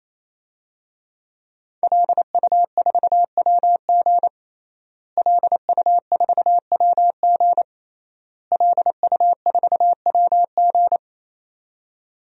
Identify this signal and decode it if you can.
Morse code, heard as a keyed tone, about 28 wpm, 700 Hz. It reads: LU4WG LU4WG LU4WG
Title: DXCC-100 Call Signs - 5 Characters - Repeated 3x - 28wpm